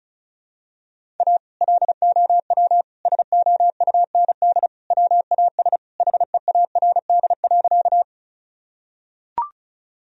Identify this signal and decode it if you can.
Morse code, heard as a keyed tone, about 34 wpm, 700 Hz. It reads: A LOW SOUND WAS HEARD.
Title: Sentences from Top 1000 Words - Review Part 1 - 35wpm